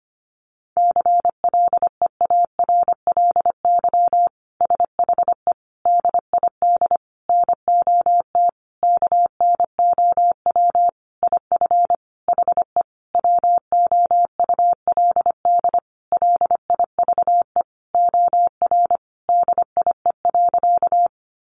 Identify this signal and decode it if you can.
Morse code, heard as a keyed tone, about 25 wpm, 700 Hz. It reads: CLEARLY SHE DID NOT KNOW IF HE WOULD LIVE OR DIE.
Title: Sentences from Top 1300 Words 25wpm